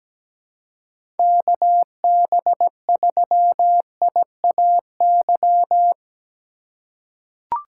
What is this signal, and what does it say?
Morse code, heard as a keyed tone, about 17 wpm, 700 Hz. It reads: KB3IAY
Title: USA Call Signs - 6 Characters - 3x Character Spacing - 17wpm